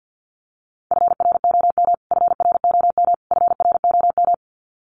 Morse code, heard as a keyed tone, about 50 wpm, 700 Hz. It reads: FROM FROM FROM
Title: Top 500 Words Repeated 50wpm